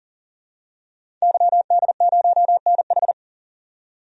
Morse code, heard as a keyed tone, about 40 wpm, 700 Hz. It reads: YD0NH
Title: DXCC-100 Call Signs - 5 Characters - 40wpm